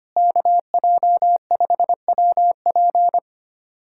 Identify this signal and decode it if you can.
Morse code, heard as a keyed tone, about 25 wpm, 690 Hz. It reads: KJ5WP
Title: SOTA QSO Practice 25wpm